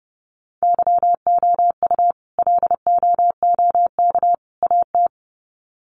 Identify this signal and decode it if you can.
Morse code, heard as a keyed tone, about 30 wpm, 700 Hz. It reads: YOU LOOK AT
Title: Sets of 3 Words 30wpm